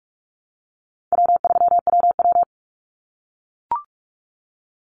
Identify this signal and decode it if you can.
Morse code, heard as a keyed tone, about 44 wpm, 700 Hz. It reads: W3WW
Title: Easy Call Signs - 45wpm